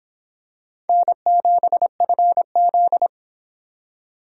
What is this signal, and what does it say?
Morse code, heard as a keyed tone, about 26 wpm, 700 Hz. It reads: N7FZ